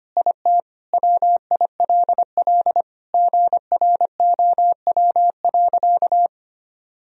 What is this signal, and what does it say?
Morse code, heard as a keyed tone, about 25 wpm, 700 Hz. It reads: IT WILL GROW.
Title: Sentences from Top 1000 Words - Review Part 2 - 25wpm